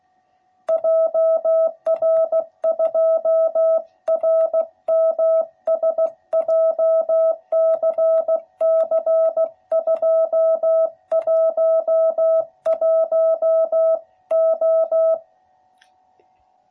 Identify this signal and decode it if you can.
Morse code, heard as a keyed tone, about 15 wpm, 660 Hz. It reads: JR2RMS JCC211O
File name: jh8cbh / 2021cw / cw12b